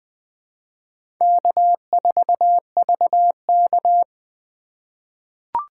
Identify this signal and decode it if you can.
Morse code, heard as a keyed tone, about 20 wpm, 700 Hz. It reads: K4VK
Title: USA Call Signs - 4 Characters - 4x Character Spacing - 20wpm